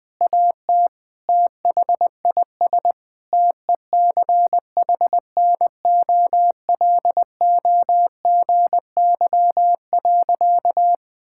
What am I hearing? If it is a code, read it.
Morse code, heard as a keyed tone, about 20 wpm, 700 Hz. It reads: AT THIS TECHNOLOGY.
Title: Sentences from Top 1000 Words - Review Part 5 - 20wpm